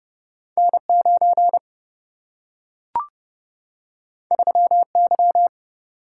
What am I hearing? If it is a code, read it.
Morse code, heard as a keyed tone, about 30 wpm, 700 Hz. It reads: N 9 3 Y
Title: Single-Letter Single-Number 30wpm